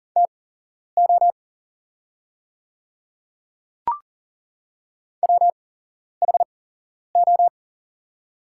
Morse code, heard as a keyed tone, about 40 wpm, spaced as for 11 wpm, 700 Hz. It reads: TO WHO
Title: 8x Character Spacing - Top 200 Words - 40wpm